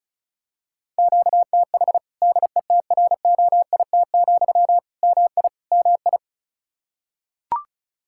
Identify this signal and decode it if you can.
Morse code, heard as a keyed tone, about 36 wpm, 700 Hz. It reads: QTH DETROIT, MI MI
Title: Practice QSO - Exchange 1 - Beginning - 35wpm